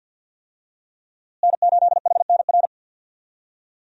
Morse code, heard as a keyed tone, about 52 wpm, 700 Hz. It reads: N8HNR
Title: USA Call Signs - 5 Characters - Repeated 3x - 50wpm